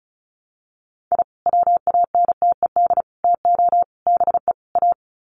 Morse code, heard as a keyed tone, about 34 wpm, 700 Hz. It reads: I WANTED TO BE A
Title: Sets of 5 Words Encore2 35wpm